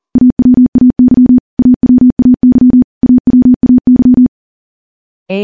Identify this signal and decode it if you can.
Morse code, heard as a keyed tone, about 40 wpm, 250 Hz. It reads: AWAY AWAY AWAY
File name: morse / morse-training-40WPM